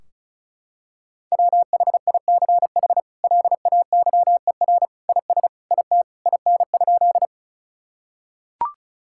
Morse code, heard as a keyed tone, about 36 wpm, 700 Hz. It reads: WHICH LAYER IS IT IN?
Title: Sentences from Top 1600 Words 35wpm